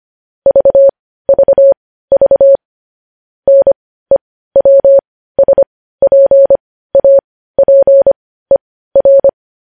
Morse code, heard as a keyed tone, about 25 wpm, spaced as for 15 wpm, 550 Hz. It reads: VVV NEWSPAPER
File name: podcast / News headlines-1x-25x15